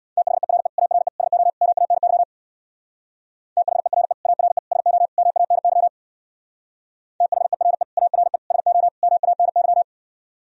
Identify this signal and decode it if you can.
Morse code, heard as a keyed tone, about 62 wpm, 700 Hz. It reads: THERE ARE SO MANY THERE ARE SO MANY THERE ARE SO MANY
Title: Speed-Racing - Sets of 4 Words 50wpm